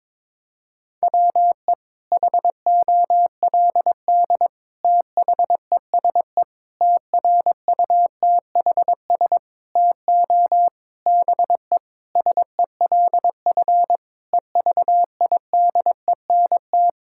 Morse code, heard as a keyed tone, about 22 wpm, 700 Hz. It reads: WE HOLD THESE TRUTHS TO BE SELF EVIDENT
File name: Sentences with 9-Word Phrases - 22wpm